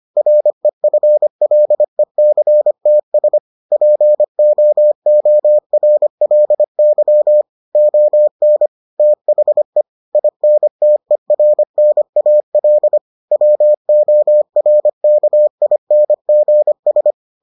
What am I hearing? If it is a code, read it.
Morse code, heard as a keyed tone, about 25 wpm, 600 Hz. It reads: REFLECTS POORLY ON THE INTERNAL WORKINGS